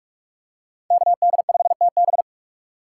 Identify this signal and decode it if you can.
Morse code, heard as a keyed tone, about 44 wpm, 700 Hz. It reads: KD5TB
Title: USA Call Signs - 5 Characters - 4x Character Spacing - 45wpm